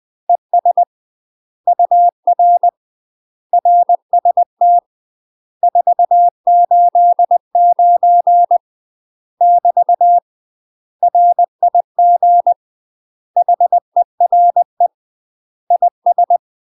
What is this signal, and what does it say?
Morse code, heard as a keyed tone, about 20 wpm, 700 Hz. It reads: ES UR RST 489 <BT> RIG HERE IS